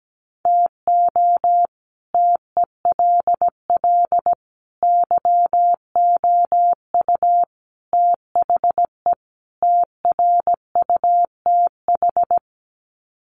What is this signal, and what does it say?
Morse code, heard as a keyed tone, about 17 wpm, 700 Hz. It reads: TO TELL YOU THE TRUTH
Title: Sets of 5 Words 17wpm